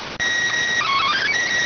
SSTV (slow-scan television), PD 290.